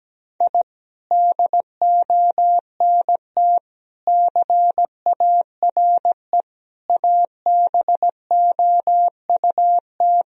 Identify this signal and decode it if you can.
Morse code, heard as a keyed tone, about 17 wpm, 700 Hz. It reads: I DONT CARE ABOUT
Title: Sentences from Top 1000 Words - Review Part 4 - 17wpm